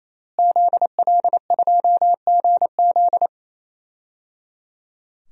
Morse code, heard as a keyed tone, about 28 wpm, 700 Hz. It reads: ZL2GZ
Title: DXCC-100 Call Signs - 5 Characters - Repeated 3x - 28wpm